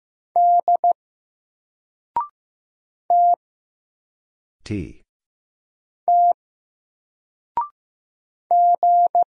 Morse code, heard as a keyed tone, about 15 wpm, 700 Hz. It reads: D T T G
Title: Single Letter-Number Alt 15wpm